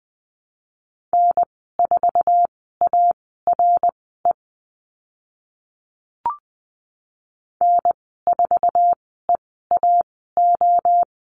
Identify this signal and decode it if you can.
Morse code, heard as a keyed tone, about 20 wpm, spaced as for 15 wpm, 700 Hz. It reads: N4ARE N4EAO
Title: 062 Lesson 11g - Letter h - Call Signs 20wpm